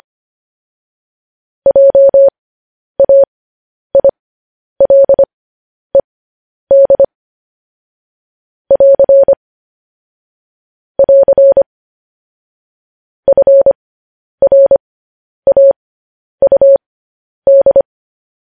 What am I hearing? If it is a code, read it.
Morse code, heard as a keyed tone, about 25 wpm, spaced as for 10 wpm, 550 Hz. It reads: JAILED <AR> <AR> FRAUD
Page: news-headlines-in-morse-code-at-10-wpm